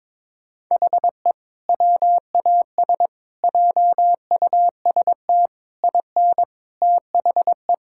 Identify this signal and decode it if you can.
Morse code, heard as a keyed tone, about 22 wpm, 700 Hz. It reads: HE WAS JUST IN THE
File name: Sentences with 7-Word Phrases - 22wpm